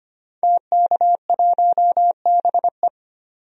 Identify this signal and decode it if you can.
Morse code, heard as a keyed tone, about 25 wpm, 700 Hz. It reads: TK1BE
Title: DXCC-100 Call Signs - 5 Characters - 25wpm